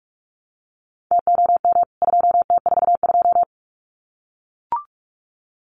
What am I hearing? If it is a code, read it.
Morse code, heard as a keyed tone, about 44 wpm, 700 Hz. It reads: TOM 2T42
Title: CWT Exchange 45wpm